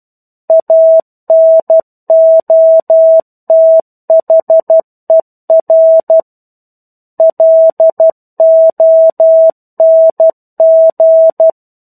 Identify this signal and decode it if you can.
Morse code, heard as a keyed tone, about 12 wpm, 650 Hz. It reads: ANOTHER LONG
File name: 01-Chapter-01 12 WPM